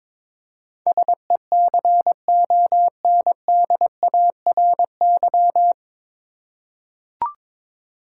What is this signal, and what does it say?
Morse code, heard as a keyed tone, about 22 wpm, 700 Hz. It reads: SECONDARY
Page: Morse Code Ninja practice 22wpm